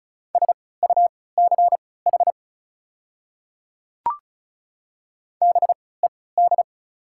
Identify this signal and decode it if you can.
Morse code, heard as a keyed tone, about 36 wpm, spaced as for 20 wpm, 700 Hz. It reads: SUCH BED
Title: 3x Character Spacing - Top 500 Words - 35wpm